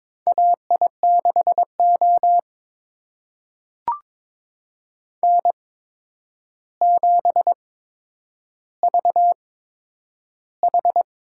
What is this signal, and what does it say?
Morse code, heard as a keyed tone, about 21 wpm, spaced as for 9 wpm, 700 Hz. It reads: AI6O N7VH
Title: USA Call Signs - 4 Characters - 8x Character Spacing - 22wpm